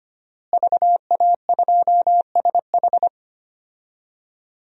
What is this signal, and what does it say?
Morse code, heard as a keyed tone, about 25 wpm, 700 Hz. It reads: VA2SH
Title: DXCC-100 Call Signs - 5 Characters - Repeated 3x - 25wpm